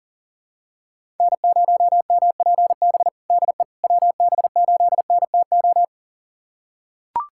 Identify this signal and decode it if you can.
Morse code, heard as a keyed tone, about 40 wpm, 700 Hz. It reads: N0MPB DE WB8NTO
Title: USA Callsign DE Callsign 40wpm